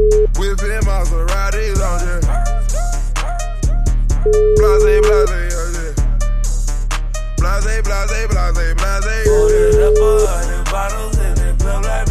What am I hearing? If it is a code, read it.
Ringback tone (Europe, 425 Hz), with noise.